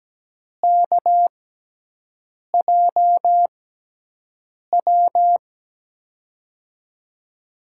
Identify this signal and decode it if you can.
Morse code, heard as a keyed tone, about 17 wpm, spaced as for 6 wpm, 700 Hz. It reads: KJW
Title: USA Call Signs - 5 Characters - 6x Character Spacing - 17wpm